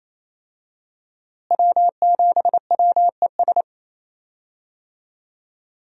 Morse code, heard as a keyed tone, about 28 wpm, 700 Hz. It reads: W7WES
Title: USA Call Signs - 5 Characters - Repeated 2x - 28wpm